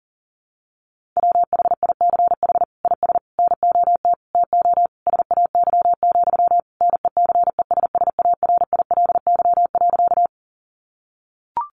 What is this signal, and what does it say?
Morse code, heard as a keyed tone, about 40 wpm, 700 Hz. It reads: WHICH IS NOT TO SAY, NECESSARILY.
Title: Sentences from Top 1400 Words 40wpm